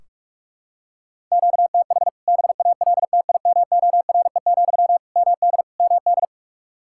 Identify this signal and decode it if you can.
Morse code, heard as a keyed tone, about 44 wpm, 700 Hz. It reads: QTH BALTIMORE, MD MD
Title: K1USN SST QSO Practice 45wpm